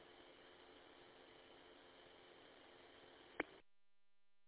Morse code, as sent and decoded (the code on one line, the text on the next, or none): none